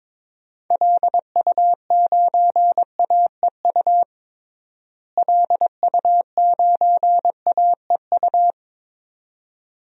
Morse code, heard as a keyed tone, about 22 wpm, 700 Hz. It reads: LU9AEU LU9AEU
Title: DXCC-100 Call Signs - 6 Characters - Repeated 2x - 22wpm